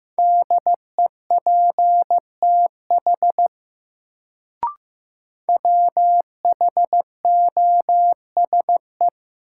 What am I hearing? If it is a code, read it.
Morse code, heard as a keyed tone, about 15 wpm, 700 Hz. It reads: DEPTH WHOSE
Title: Sentences from Top 1200 Words - New Word Drill - 15wpm